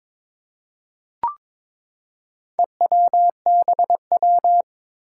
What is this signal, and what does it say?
Morse code, heard as a keyed tone, about 22 wpm, 700 Hz. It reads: EWBW
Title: Four Challenging Letter-Number Combinations 22wpm